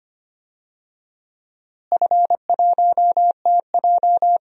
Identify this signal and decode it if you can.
Morse code, heard as a keyed tone, about 25 wpm, 700 Hz. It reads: F1TJ
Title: DXCC-100 Call Signs - 4 Characters - Repeated 2x - 25wpm